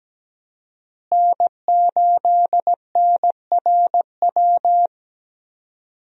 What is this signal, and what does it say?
Morse code, heard as a keyed tone, about 17 wpm, 700 Hz. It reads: N8NRW
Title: USA Call Signs - 5 Characters - Repeated 2x - 17wpm